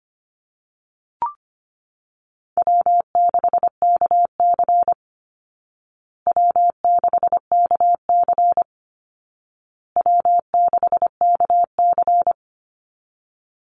Morse code, heard as a keyed tone, about 25 wpm, 700 Hz. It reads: W6KC W6KC W6KC
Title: Easy Call Signs Repeated 3x - 25wpm